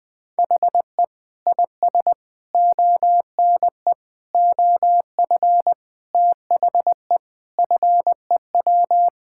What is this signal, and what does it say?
Morse code, heard as a keyed tone, about 20 wpm, 700 Hz. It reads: HE IS ONE OF THE FEW
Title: Sentences with 4-Word Phrases - Spoken Beginning - 20wpm